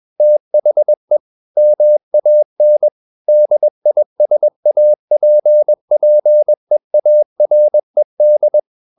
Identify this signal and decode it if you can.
Morse code, heard as a keyed tone, about 21 wpm, 600 Hz. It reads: THE MAN DISAPPEARED
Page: CW med Gnister